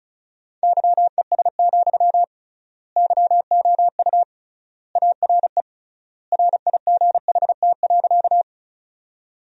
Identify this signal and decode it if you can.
Morse code, heard as a keyed tone, about 36 wpm, 700 Hz. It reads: YES, YOU ARE RIGHT.
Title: 3x Word-Spacing - Sentences from Top 500 Words - 35wpm